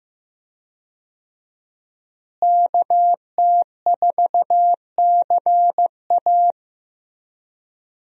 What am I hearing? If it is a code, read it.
Morse code, heard as a keyed tone, about 15 wpm, 700 Hz. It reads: KT4CA
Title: USA Call Signs - 5 Characters - Repeated 2x - 15wpm